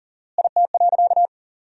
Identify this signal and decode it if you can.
Morse code, heard as a keyed tone, about 40 wpm, 700 Hz. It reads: IT.
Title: 4x Word-Spacing - Sentences from Top 500 Words - 40wpm